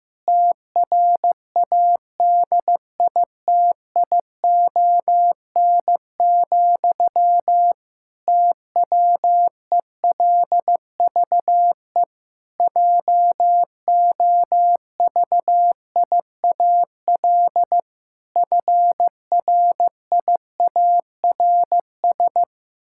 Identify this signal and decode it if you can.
Morse code, heard as a keyed tone, about 15 wpm, 700 Hz. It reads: TRADITION, TWELVE JOVIAL FRIARS
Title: Pangram Challenge v2 - 15wpm